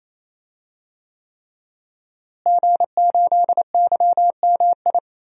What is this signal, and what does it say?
Morse code, heard as a keyed tone, about 27 wpm, 700 Hz. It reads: G8YMI
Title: DXCC-100 Call Signs - 5 Characters - Repeated 3x - 28wpm